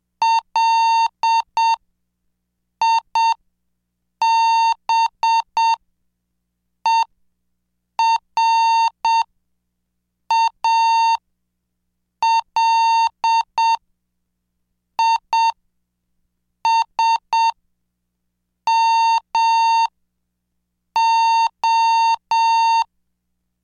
Morse code, heard as a keyed tone, about 7 wpm, spaced as for 5 wpm, 910 Hz. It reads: LIBERALISMO
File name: cursos / CDestudo / 26 Track 26